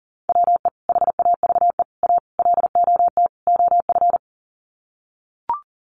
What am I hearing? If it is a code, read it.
Morse code, heard as a keyed tone, about 40 wpm, 700 Hz. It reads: WE HAVE A LOT OF
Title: Sets of 5 Words 40wpm